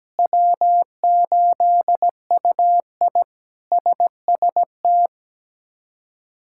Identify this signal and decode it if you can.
Morse code, heard as a keyed tone, about 17 wpm, 700 Hz. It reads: W8UI SST
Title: K1USN SST QSO Practice - 17wpm